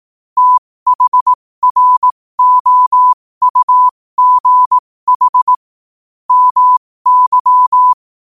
Morse code, heard as a keyed tone, about 18 wpm, spaced as for 15 wpm, 1000 Hz. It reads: THROUGH MY
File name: qotdfiles / 15wpm